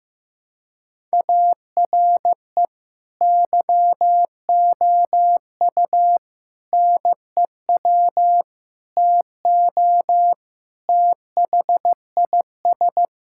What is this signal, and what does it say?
Morse code, heard as a keyed tone, about 16 wpm, 700 Hz. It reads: ARE YOU NEW TO THIS
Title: Sentences from Top 1000 Words - Review Part 6 - 15wpm